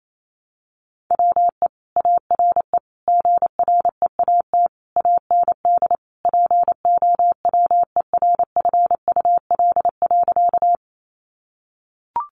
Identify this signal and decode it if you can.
Morse code, heard as a keyed tone, about 28 wpm, 700 Hz. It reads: WE ARE GREAT AND POWERFUL.
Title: Sentences from Top 1300 Words 28wpm